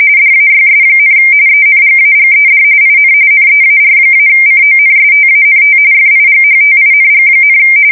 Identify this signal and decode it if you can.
RTTY, about 45.45 baud. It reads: DAYS, ENGAGED IN NEAR-CONSTANT FIGHTING,
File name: fsk170 / battleshipnj